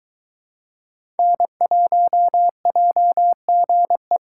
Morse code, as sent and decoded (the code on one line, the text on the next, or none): -. .---- .--- --. .
N1JGE